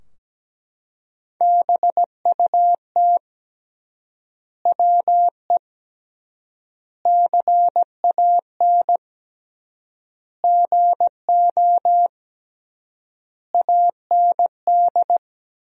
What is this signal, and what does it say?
Morse code, heard as a keyed tone, about 17 wpm, 700 Hz. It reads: BUT WE CAN GO AND
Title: 3x Word-Spacing - Sentences from Top 100 Words - 17wpm